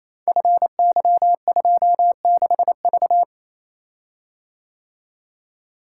Morse code, heard as a keyed tone, about 27 wpm, 700 Hz. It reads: FY26V
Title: Code Groups - 5 Letter:Numbers - 28wpm